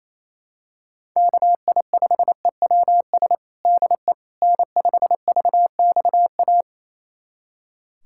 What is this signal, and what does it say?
Morse code, heard as a keyed tone, about 28 wpm, 700 Hz. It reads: KI5EWS DE N5VXA
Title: USA Callsign DE Callsign 28wpm